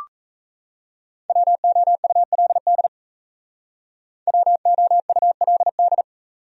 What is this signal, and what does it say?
Morse code, heard as a keyed tone, about 34 wpm, 700 Hz. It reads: WOULD WOULD
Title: Speed-Racing Top 200 Words 28wpm